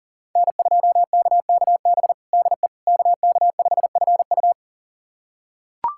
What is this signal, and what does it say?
Morse code, heard as a keyed tone, about 38 wpm, 700 Hz. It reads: N2KKB DE KK5FU
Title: USA Callsign DE Callsign 40wpm